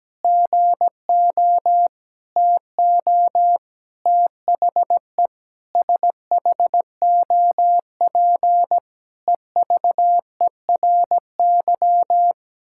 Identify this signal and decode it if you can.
Morse code, heard as a keyed tone, about 17 wpm, 700 Hz. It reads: GO TO THE SHOP EVERY
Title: Sentences from Top 1600 Words 17wpm